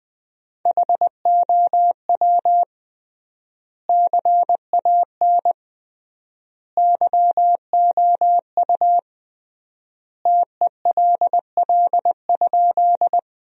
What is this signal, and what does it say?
Morse code, heard as a keyed tone, about 21 wpm, 700 Hz. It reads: HOW CAN YOU TELL?
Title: 3x Word-Spacing - Sentences from Top 200 Words - 20wpm